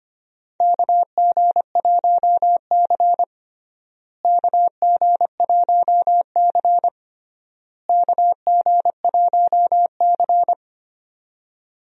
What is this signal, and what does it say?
Morse code, heard as a keyed tone, about 25 wpm, 700 Hz. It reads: KG1C KG1C KG1C
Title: USA Call Signs - 4 Characters - Repeated 3x - 25wpm